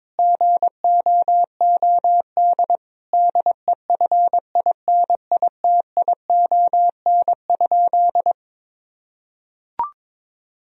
Morse code, heard as a keyed tone, about 22 wpm, 700 Hz. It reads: GOOD DEFINITION?
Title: Sentences from Top 1000 Words - Review Part 4 - 22wpm